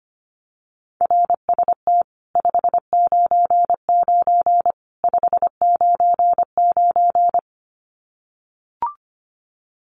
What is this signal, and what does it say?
Morse code, heard as a keyed tone, about 25 wpm, 700 Hz. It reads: RST 599 599
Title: Common QSO Phrases 25wpm